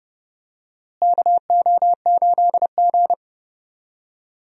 Morse code, heard as a keyed tone, about 30 wpm, 700 Hz. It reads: KO8G
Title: USA Call Signs - 4 Characters - Repeated 2x - 30wpm